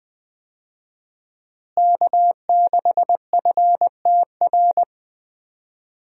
Morse code, heard as a keyed tone, about 20 wpm, 700 Hz. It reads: K6FTR
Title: USA Call Signs - 5 Characters - Repeated 3x - 20wpm